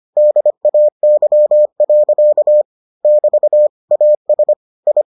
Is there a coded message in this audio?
Yes: Morse code from 0.2 to 5.0 s, heard as a keyed tone, about 25 wpm, 600 Hz. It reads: DAY. <BT> AS I